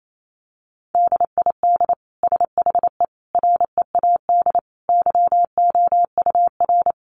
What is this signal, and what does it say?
Morse code, heard as a keyed tone, about 28 wpm, 700 Hz. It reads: DID SHE READ YOUR
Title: Sentences from Top 1400 Words 28wpm